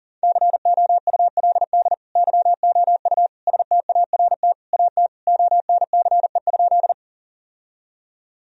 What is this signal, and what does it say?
Morse code, heard as a keyed tone, about 40 wpm, 700 Hz. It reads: COULD YOU START AT ONCE?
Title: Sentences from Top 1000 Words - Review Part 1 - 40wpm